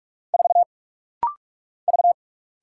Morse code, heard as a keyed tone, about 46 wpm, 700 Hz. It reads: 4 V
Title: Single Letter-Number - Rapid-Fire 45wpm